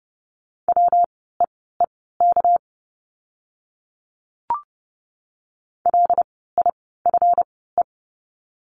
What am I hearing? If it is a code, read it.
Morse code, heard as a keyed tone, about 30 wpm, spaced as for 17 wpm, 700 Hz. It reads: WEEK LIFE
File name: course / 30:10wpm / 230 Lesson 37f - Letter x - Words 30wpm